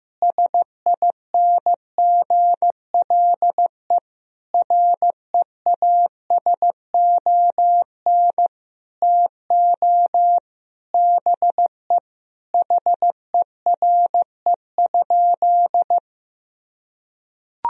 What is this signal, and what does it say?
Morse code, heard as a keyed tone, about 15 wpm, 700 Hz. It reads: SINGLE REASON TO BE HERE?
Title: Sentences from Top 1000 Words - Review Part 4 - 15wpm